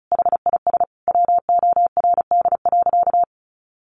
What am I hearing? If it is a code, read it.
Morse code, heard as a keyed tone, about 34 wpm, 700 Hz. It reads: HIS WORD.
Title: Sentences from Top 1400 Words 35wpm